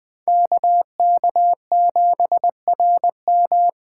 Morse code, heard as a keyed tone, about 20 wpm, 700 Hz. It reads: KK7RM
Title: USA Call Signs - 5 Characters - 4x Character Spacing - 20wpm